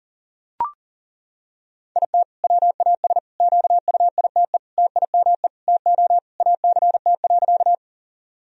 Morse code, heard as a keyed tone, about 40 wpm, 700 Hz. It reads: IT WAS QUITE TIME TO ACT.
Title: Sentences from Top 1000 Words - Review Part 1 - 40wpm